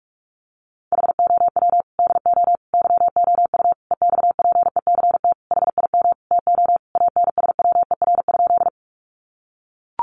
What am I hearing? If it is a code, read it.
Morse code, heard as a keyed tone, about 44 wpm, 700 Hz. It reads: HOW DO YOU EXPECT HIM TO ANSWER?
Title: Sentences from Top 1100 Words 45wpm